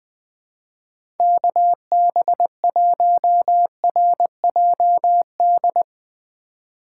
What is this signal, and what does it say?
Morse code, heard as a keyed tone, about 20 wpm, 700 Hz. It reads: KB1RJD
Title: 2023 SOTA Chaser Call Signs - 20wpm